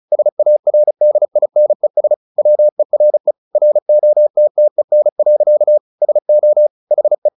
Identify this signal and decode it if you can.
Morse code, heard as a keyed tone, about 36 wpm, 600 Hz. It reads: SARDINES WERE ROTTEN. SO HE
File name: the old man and the sea 2 35wpm